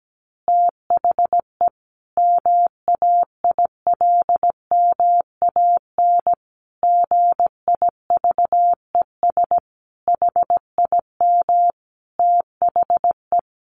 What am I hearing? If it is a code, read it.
Morse code, heard as a keyed tone, about 17 wpm, 700 Hz. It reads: THE MAILMAN GIVES HIM THE
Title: Additional Easy Phrases 17wpm